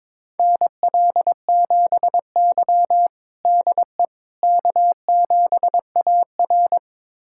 Morse code, heard as a keyed tone, about 22 wpm, 700 Hz. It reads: NL7Y DE K7AR
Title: USA Callsign DE Callsign - 4 Characters - 22wpm